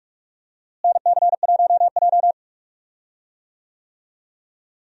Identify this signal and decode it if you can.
Morse code, heard as a keyed tone, about 46 wpm, 700 Hz. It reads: NC1J
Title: USA Call Signs - 4 Characters - Repeated 3x - 45wpm